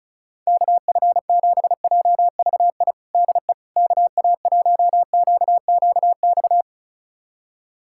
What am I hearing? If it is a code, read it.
Morse code, heard as a keyed tone, about 36 wpm, 700 Hz. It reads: KF7JVI DE KA1QQX
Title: USA Callsign DE Callsign - 6 Characters - 35wpm